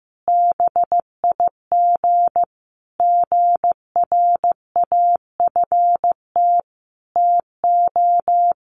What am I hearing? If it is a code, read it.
Morse code, heard as a keyed tone, about 15 wpm, 700 Hz. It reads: BIG GRAFT TO
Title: A Sound Of Thunder 15wpm